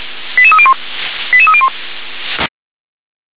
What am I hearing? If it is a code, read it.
Five-tone selective call (ZVEI), heard as quick runs of five tones, tones of about 70 ms. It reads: 88301 88301
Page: /357/SDR/ZVEI/88301/2022.11.15/